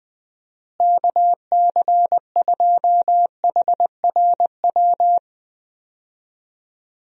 Morse code, heard as a keyed tone, about 21 wpm, 700 Hz. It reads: KC2HRW
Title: USA Call Signs - 6 Characters - Repeated 3x - 20wpm